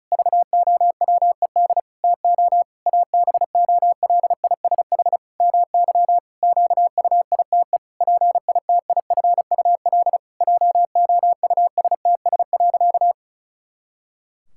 Morse code, heard as a keyed tone, about 36 wpm, 700 Hz. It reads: VOWED TO ABOLISH MY QUITE PITIFUL JOUSTS.
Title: Pangram Challenge v2 - 35wpm